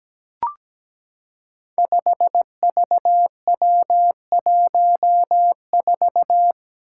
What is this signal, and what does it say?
Morse code, heard as a keyed tone, about 17 wpm, 700 Hz. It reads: 5VW14
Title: Five Challenging Letter-Number Combinations 17wpm